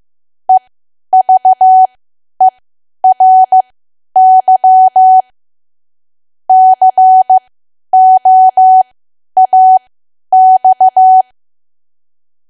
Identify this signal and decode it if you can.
Morse code, heard as a keyed tone, about 15 wpm, spaced as for 10 wpm, 750 Hz. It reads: EVERY COAX
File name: Archive / 10 WPM / 10WPM